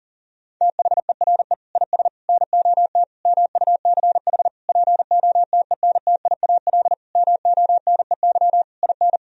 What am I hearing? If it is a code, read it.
Morse code, heard as a keyed tone, about 40 wpm, 700 Hz. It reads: THERE IS NOT MUCH POTENTIAL MONEY IN
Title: Sentences from Top 1000 Words - Review Part 4 - 40wpm